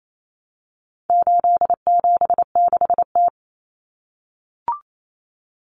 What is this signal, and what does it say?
Morse code, heard as a keyed tone, about 28 wpm, 700 Hz. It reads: 876T
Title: Common Cut Numbers - 4-Digit 28wpm